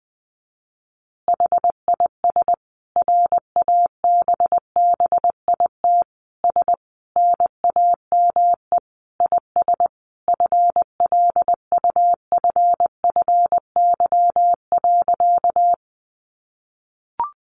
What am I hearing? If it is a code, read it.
Morse code, heard as a keyed tone, about 20 wpm, 700 Hz. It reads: HIS RABBIT S NAME IS FLUFFY.
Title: Easy Phrases 20wpm